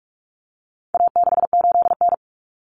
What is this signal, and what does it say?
Morse code, heard as a keyed tone, about 44 wpm, 700 Hz. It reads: A68N
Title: Common Cut Numbers - 4-Digit 45wpm